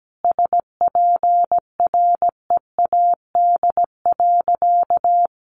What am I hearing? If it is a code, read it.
Morse code, heard as a keyed tone, about 17 wpm, 700 Hz. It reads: SPREAD.
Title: Sentences from Top 1200 Words 17wpm